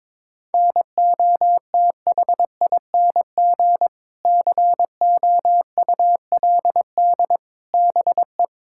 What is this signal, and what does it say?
Morse code, heard as a keyed tone, about 21 wpm, 700 Hz. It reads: NOTHING COULD BE
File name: practice / 22wpm / Sets of 7 Words - 22wpm